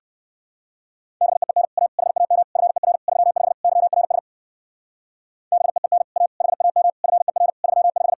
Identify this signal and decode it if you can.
Morse code, heard as a keyed tone, about 58 wpm, 700 Hz. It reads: BEEN A HAM FER 35 YRS BEEN A HAM FER 35